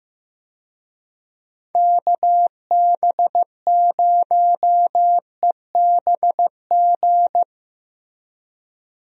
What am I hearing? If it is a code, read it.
Morse code, heard as a keyed tone, about 15 wpm, 700 Hz. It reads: KB0EBG
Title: USA Call Signs - 6 Characters - Repeated 2x - 15wpm